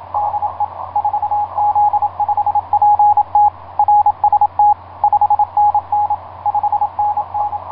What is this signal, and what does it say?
Morse code, heard as a keyed tone, about 26 wpm, 870 Hz, with radio noise. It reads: DE 4Z5PT RST 5NN 5NN
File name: 8q7qx dk7lx etc 12m cw 30mar2010 1141